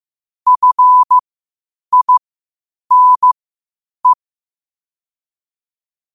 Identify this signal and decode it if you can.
Morse code, heard as a keyed tone, about 15 wpm, spaced as for 8 wpm, 1000 Hz. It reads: FINE
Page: qotd-at-8-wpm